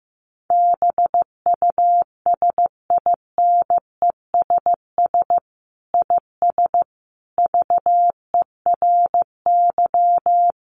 Morse code, heard as a keyed tone, about 14 wpm, 700 Hz. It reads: BUSINESS IS VERY
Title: Sentences from Top 700 Words 15wpm